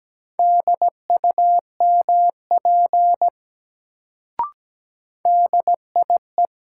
Morse code, heard as a keyed tone, about 17 wpm, 700 Hz. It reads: DUMP DIE
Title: Sentences from Top 1300 Words - New Word Drill Alt - 17wpm